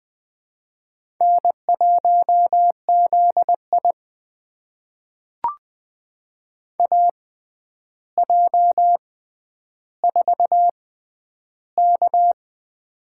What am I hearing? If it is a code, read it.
Morse code, heard as a keyed tone, about 20 wpm, spaced as for 10 wpm, 700 Hz. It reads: N1ZI AJ4K
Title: USA Call Signs - 4 Characters - 6x Character Spacing - 20wpm